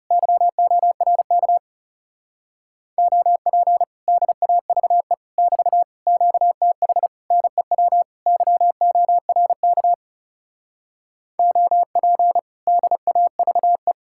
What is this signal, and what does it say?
Morse code, heard as a keyed tone, about 40 wpm, 700 Hz. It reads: YORK OP DAVE <BT> QTH NEW YORK OP DAVE